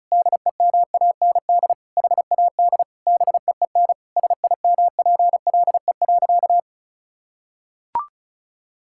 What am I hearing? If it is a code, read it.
Morse code, heard as a keyed tone, about 34 wpm, 700 Hz. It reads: DEMAND HAD BEEN SIMPLE.
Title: Sentences from Top 1000 Words - Review Part 6 - 35wpm